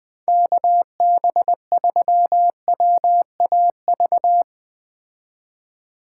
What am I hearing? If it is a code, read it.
Morse code, heard as a keyed tone, about 20 wpm, 700 Hz. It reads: KB3WAV
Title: 2023 POTA Activator Call Signs - 20wpm